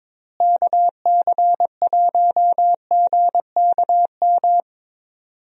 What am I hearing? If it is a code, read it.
Morse code, heard as a keyed tone, about 22 wpm, 700 Hz. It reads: KC1GKM
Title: USA Call Signs - 6 Characters - Repeated 2x - 22wpm